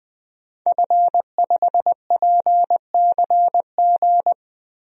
Morse code, heard as a keyed tone, about 21 wpm, 700 Hz. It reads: F5PCG
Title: DXCC-100 Call Signs - 5 Characters - 20wpm